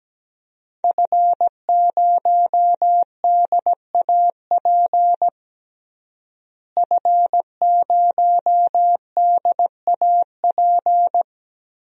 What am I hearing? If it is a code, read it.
Morse code, heard as a keyed tone, about 17 wpm, 700 Hz. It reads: F0DAP F0DAP
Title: DXCC-100 Call Signs - 5 Characters - Repeated 2x - 17wpm